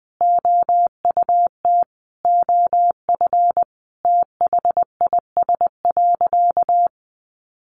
Morse code, heard as a keyed tone, about 19 wpm, 700 Hz. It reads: OUT OF THIS.
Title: A Sound Of Thunder 20wpm